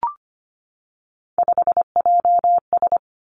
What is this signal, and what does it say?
Morse code, heard as a keyed tone, about 25 wpm, 700 Hz. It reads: 5JS
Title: Three Challenging Letter-Number Combinations 25wpm